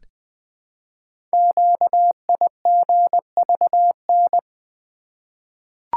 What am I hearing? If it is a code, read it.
Morse code, heard as a keyed tone, about 20 wpm, 700 Hz. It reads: QIGVN
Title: Code Groups - 5 Letters - 20wpm